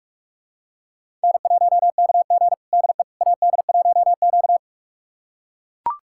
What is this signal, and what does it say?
Morse code, heard as a keyed tone, about 41 wpm, 700 Hz. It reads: N1KG DE AD1Q
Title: USA Callsign DE Callsign - 4 Characters - 45wpm